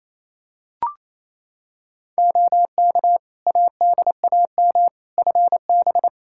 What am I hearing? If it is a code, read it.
Morse code, heard as a keyed tone, about 28 wpm, 700 Hz. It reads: OK ADAM FB
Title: Practice QSO - Exchange 2 - Info - 28wpm